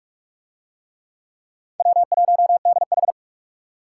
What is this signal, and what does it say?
Morse code, heard as a keyed tone, about 43 wpm, 700 Hz. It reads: W1DH